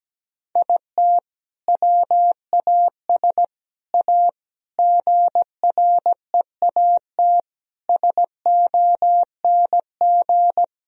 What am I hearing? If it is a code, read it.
Morse code, heard as a keyed tone, about 17 wpm, 700 Hz. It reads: IT WAS A GREAT SONG
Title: Sentences from Top 1000 Words - Review Part 4 - 17wpm